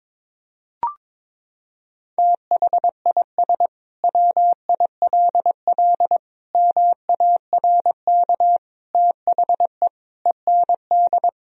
Morse code, heard as a keyed tone, about 22 wpm, 700 Hz. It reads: THIS WILL MARK THE END
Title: Sentences from Top 1000 Words - Review Part 1 - 22wpm